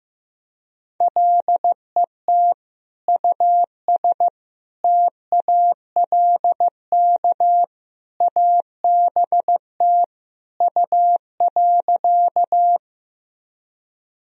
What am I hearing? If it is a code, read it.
Morse code, heard as a keyed tone, about 15 wpm, 700 Hz. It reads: LET US TALK ABT U.
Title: Sentences from Top 400 Words and Abbreviations 15wpm